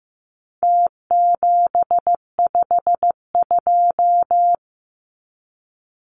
Morse code, heard as a keyed tone, about 15 wpm, 700 Hz. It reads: T752
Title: Numbers - Contest Style - 4-Digit 15wpm